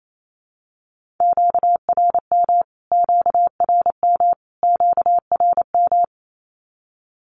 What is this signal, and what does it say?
Morse code, heard as a keyed tone, about 28 wpm, 700 Hz. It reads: QRM QRM QRM
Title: Q-Codes Repeated 3x 28wpm